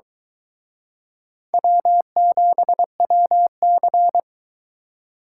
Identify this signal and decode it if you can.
Morse code, heard as a keyed tone, about 22 wpm, 700 Hz. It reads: W7WC